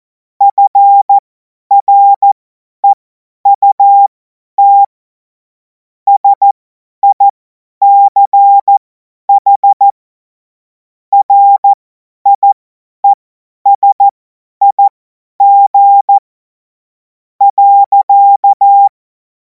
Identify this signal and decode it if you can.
Morse code, heard as a keyed tone, about 14 wpm, spaced as for 9 wpm, 800 Hz. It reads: FREUT SICH RIESIG.
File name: Morsekurs / Witze / Witz 09. Juni 2026 50 F70